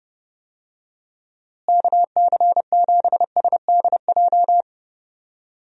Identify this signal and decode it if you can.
Morse code, heard as a keyed tone, about 32 wpm, 700 Hz. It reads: KC7SDJ
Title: USA Call Signs - 6 Characters - Repeated 3x - 30wpm